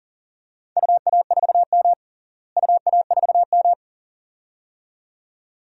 Morse code, heard as a keyed tone, about 40 wpm, 700 Hz. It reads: UA4M UA4M
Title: DXCC-100 Call Signs - 4 Characters - Repeated 2x - 40wpm